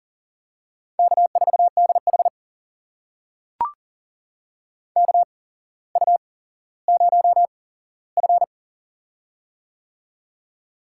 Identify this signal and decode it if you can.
Morse code, heard as a keyed tone, about 39 wpm, spaced as for 17 wpm, 700 Hz. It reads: K4DH KU0F
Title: USA Call Signs - 4 Characters - 8x Character Spacing - 40wpm